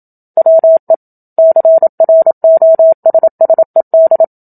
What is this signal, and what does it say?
Morse code, heard as a keyed tone, about 27 wpm, 650 Hz. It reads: WE CROSSED
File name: Book Two / 01-Chapter 1 - 27 WPM